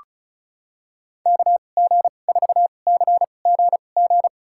Morse code, heard as a keyed tone, about 36 wpm, spaced as for 25 wpm, 700 Hz. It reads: KG4CGG
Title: USA Call Signs - 6 Characters - 2x Character Spacing - 35wpm